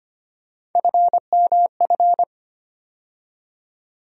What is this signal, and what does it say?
Morse code, heard as a keyed tone, about 25 wpm, 700 Hz. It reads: FMF
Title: USA Call Sign Postfixes - 3 Letters 25wpm